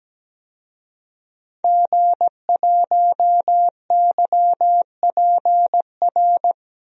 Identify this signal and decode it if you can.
Morse code, heard as a keyed tone, about 17 wpm, 700 Hz. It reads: G1YPR
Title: DXCC-100 Call Signs - 5 Characters - 17wpm